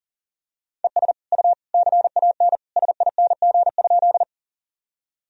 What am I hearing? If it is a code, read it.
Morse code, heard as a keyed tone, about 40 wpm, 700 Hz. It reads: ES U CAN SING?
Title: Sentences from Top 400 Words and Abbreviations 40wpm